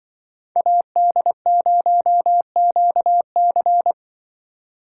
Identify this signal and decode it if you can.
Morse code, heard as a keyed tone, about 25 wpm, 700 Hz. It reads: AD0QC